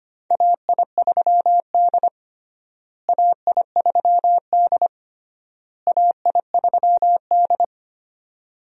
Morse code, heard as a keyed tone, about 25 wpm, 700 Hz. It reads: AI3D AI3D AI3D
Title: USA Call Signs - 4 Characters - Repeated 3x - 25wpm